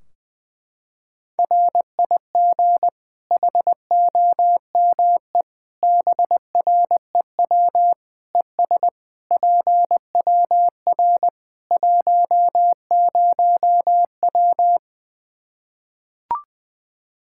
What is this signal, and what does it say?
Morse code, heard as a keyed tone, about 20 wpm, 700 Hz. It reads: RIG HOME BREW ES PWR 10W
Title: Practice QSO - Exchange 2 - Info - 20wpm